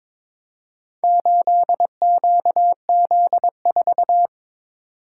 Morse code, heard as a keyed tone, about 22 wpm, 700 Hz. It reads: 8QZ4
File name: practice / 22wpm / Four Letter-Number Combinations - 22wpm